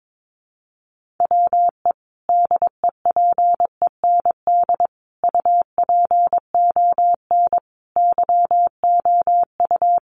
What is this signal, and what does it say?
Morse code, heard as a keyed tone, about 22 wpm, 700 Hz. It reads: WE DEPEND UPON YOU
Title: Sentences from Top 1300 Words 22wpm